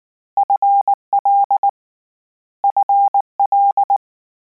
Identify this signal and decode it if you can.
Morse code, heard as a keyed tone, about 19 wpm, 800 Hz. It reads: FL FL